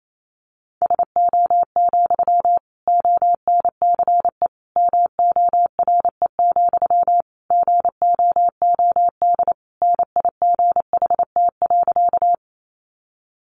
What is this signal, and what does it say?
Morse code, heard as a keyed tone, about 28 wpm, 700 Hz. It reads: SO, ONCE MORE, GOOD NIGHT.
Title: Sentences from Top 300 Words 28wpm